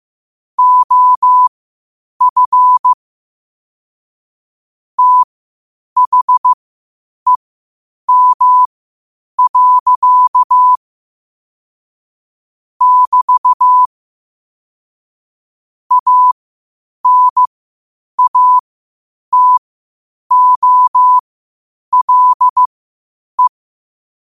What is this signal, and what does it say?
Morse code, heard as a keyed tone, about 15 wpm, spaced as for 8 wpm, 1000 Hz. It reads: OF THEM. <BT> ANATOLE
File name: 8wpm